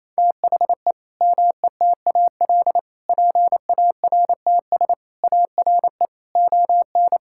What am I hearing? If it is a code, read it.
Morse code, heard as a keyed tone, about 28 wpm, 700 Hz. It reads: THE METAL PARTS ARE ON
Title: Sentences from Top 1000 Words - Review Part 6 - 28wpm